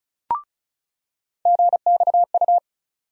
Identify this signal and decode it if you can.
Morse code, heard as a keyed tone, about 34 wpm, 700 Hz. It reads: GXU
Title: USA Call Sign Postfixes - 3 Letters 35wpm